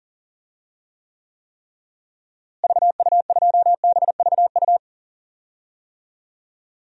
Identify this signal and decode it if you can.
Morse code, heard as a keyed tone, about 40 wpm, 700 Hz. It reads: VU2BVU
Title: DXCC-100 Call Signs - 6 Characters - Repeated 3x - 40wpm